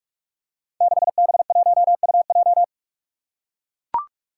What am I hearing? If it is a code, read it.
Morse code, heard as a keyed tone, about 44 wpm, 700 Hz. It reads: 6B1UJ
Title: Five Challenging Letter-Number Combinations 45wpm